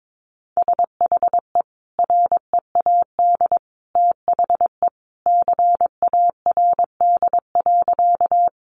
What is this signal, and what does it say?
Morse code, heard as a keyed tone, about 22 wpm, 700 Hz. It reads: SHE READ THE CARD.
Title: Sentences from Top 600 Words 22wpm